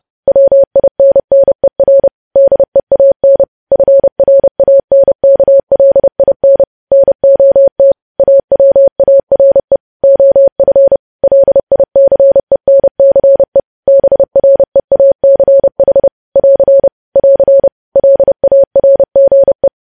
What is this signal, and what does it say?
Morse code, heard as a keyed tone, about 30 wpm, 550 Hz. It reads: WINNER DEAN FRANKLIN NOT AWARE OF LICENCE BREACH <AR> <AR> LARGE
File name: podcast / News headlines-1x-30x30